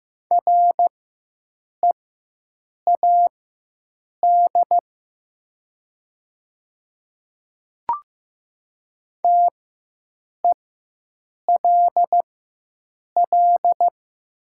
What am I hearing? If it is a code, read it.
Morse code, heard as a keyed tone, about 15 wpm, spaced as for 7 wpm, 690 Hz. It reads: READ TELL
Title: 4x Character Spacing - Top 500 Words - 15wpm